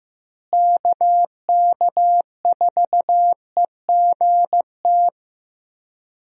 Morse code, heard as a keyed tone, about 15 wpm, 700 Hz. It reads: KK4EGT
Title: USA Call Signs - 6 Characters - Repeated 3x - 15wpm